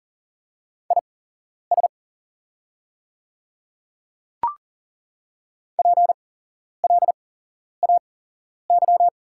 Morse code, heard as a keyed tone, about 40 wpm, spaced as for 11 wpm, 690 Hz. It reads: IS PLAY
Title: 8x Character Spacing - Top 300 Words - 40wpm